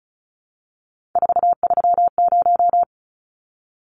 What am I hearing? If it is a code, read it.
Morse code, heard as a keyed tone, about 34 wpm, 700 Hz. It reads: 430
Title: Numbers - 3-Digit 35wpm